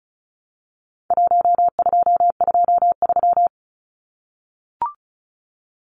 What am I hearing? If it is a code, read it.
Morse code, heard as a keyed tone, about 34 wpm, 700 Hz. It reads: 1223
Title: Numbers - Contest Style - 4-Digit 35wpm